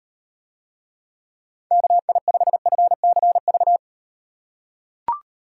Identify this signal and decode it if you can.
Morse code, heard as a keyed tone, about 38 wpm, 700 Hz. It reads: KI5FCV